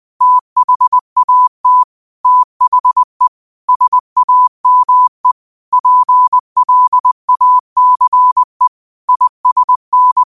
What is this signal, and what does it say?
Morse code, heard as a keyed tone, about 20 wpm, 1000 Hz. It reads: THAT THE SAME PLACE ISN
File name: qotdfiles / 20wpm